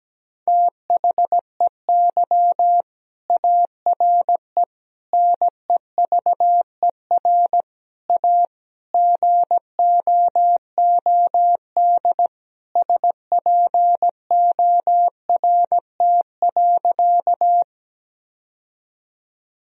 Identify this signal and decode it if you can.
Morse code, heard as a keyed tone, about 17 wpm, 700 Hz. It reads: THEY ARE NEVER A GOOD SPORT.
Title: Sentences from Top 1000 Words - Review Part 4 - 17wpm